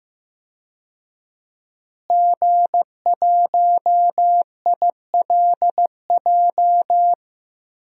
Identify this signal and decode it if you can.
Morse code, heard as a keyed tone, about 15 wpm, 700 Hz. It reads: G1ILJ
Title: DXCC-100 Call Signs - 5 Characters - Repeated 2x - 15wpm